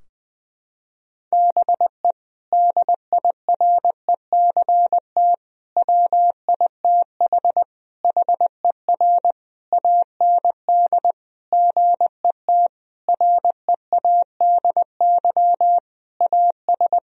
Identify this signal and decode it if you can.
Morse code, heard as a keyed tone, about 20 wpm, 700 Hz. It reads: BE DIRECT WITH HER AND GET READY AS
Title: Sentences from Top 1000 Words - Review Part 2 - 20wpm